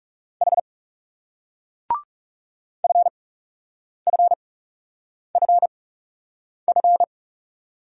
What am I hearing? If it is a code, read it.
Morse code, heard as a keyed tone, about 39 wpm, spaced as for 8 wpm, 700 Hz. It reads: H FFFF